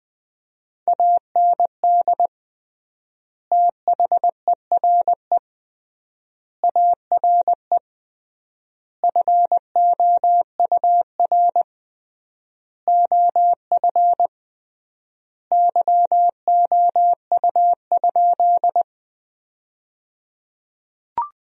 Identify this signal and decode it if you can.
Morse code, heard as a keyed tone, about 20 wpm, 700 Hz. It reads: AND THERE ARE FOUR OF YOU?
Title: 3x Word-Spacing - Sentences from Top 300 Words - 20wpm